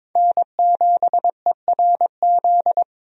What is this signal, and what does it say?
Morse code, heard as a keyed tone, about 22 wpm, 700 Hz. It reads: N7ERZ
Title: USA Call Signs - 5 Characters - 22wpm